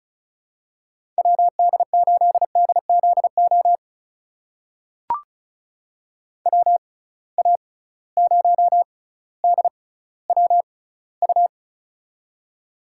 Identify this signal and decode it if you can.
Morse code, heard as a keyed tone, about 36 wpm, spaced as for 18 wpm, 700 Hz. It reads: WD8DZO WA0DWU
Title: USA Call Signs - 6 Characters - 6x Character Spacing - 35wpm